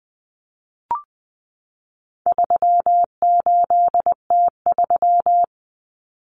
Morse code, heard as a keyed tone, about 20 wpm, 700 Hz. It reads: 38T3
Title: Common Cut Numbers - 4-Digit 20wpm